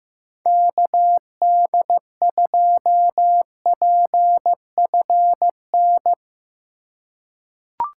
Morse code, heard as a keyed tone, about 15 wpm, 700 Hz. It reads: KD2PFN
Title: USA Call Signs - 6 Characters - 4x Character Spacing - 15wpm